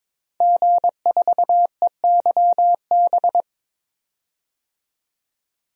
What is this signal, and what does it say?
Morse code, heard as a keyed tone, about 22 wpm, 700 Hz. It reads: G4EYB
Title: DXCC-100 Call Signs - 5 Characters - 22wpm